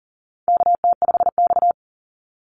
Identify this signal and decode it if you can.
Morse code, heard as a keyed tone, about 40 wpm, 700 Hz. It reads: KT5X
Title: NA SOTA Activator Callsigns 40wpm